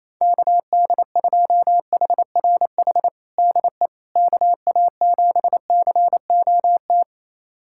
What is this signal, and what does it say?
Morse code, heard as a keyed tone, about 28 wpm, 700 Hz. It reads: KD2HRH DE KA7COT
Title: USA Callsign DE Callsign 28wpm